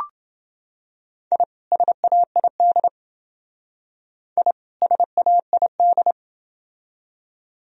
Morse code, heard as a keyed tone, about 28 wpm, 700 Hz. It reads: I SAID I SAID